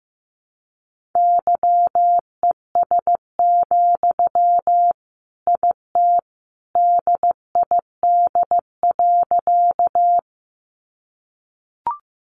Morse code, heard as a keyed tone, about 15 wpm, 700 Hz. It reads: YES, IT DID.
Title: Sentences from Top 500 Words 15wpm